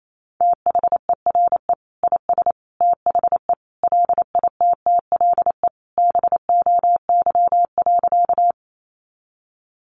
Morse code, heard as a keyed tone, about 28 wpm, 700 Hz. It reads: THERE IS THE LITTLE BOY.
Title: Sentences from Top 200 Words 28wpm